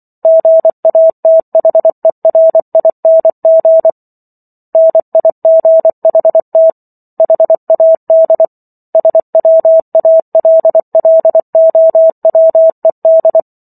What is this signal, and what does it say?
Morse code, heard as a keyed tone, about 24 wpm, 650 Hz. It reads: GATHERING NIGHT HAD SWALLOWED
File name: Book One / 15-Chapter 15 - 24 WPM